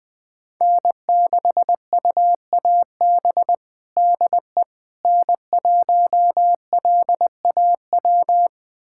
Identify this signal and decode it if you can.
Morse code, heard as a keyed tone, about 21 wpm, 700 Hz. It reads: N6UAB DE N1LAW
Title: USA Callsign DE Callsign - 5 Characters - 20wpm